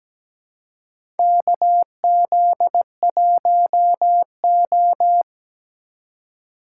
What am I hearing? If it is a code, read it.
Morse code, heard as a keyed tone, about 17 wpm, 700 Hz. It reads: KZ1O
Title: USA Call Signs - 4 Characters - 2x Character Spacing - 17wpm